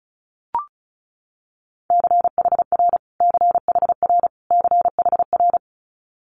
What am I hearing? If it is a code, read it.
Morse code, heard as a keyed tone, about 36 wpm, 700 Hz. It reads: CHR CHR CHR
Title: Multi-letter Phonemes Repeated 3x 35wpm